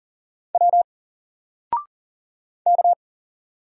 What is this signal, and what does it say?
Morse code, heard as a keyed tone, about 40 wpm, 700 Hz. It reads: W K
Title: Single Letter-Number 40wpm